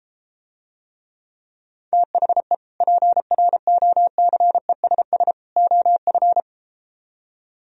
Morse code, heard as a keyed tone, about 33 wpm, 700 Hz. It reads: THE PROCESS OF